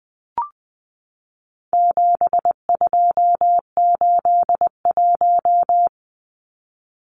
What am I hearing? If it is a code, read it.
Morse code, heard as a keyed tone, about 20 wpm, 700 Hz. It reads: 7281
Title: Numbers - 4-Digit 20wpm